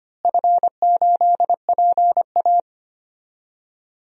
Morse code, heard as a keyed tone, about 25 wpm, 700 Hz. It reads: F8PA
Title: DXCC-100 Call Signs - 4 Characters - Repeated 2x - 25wpm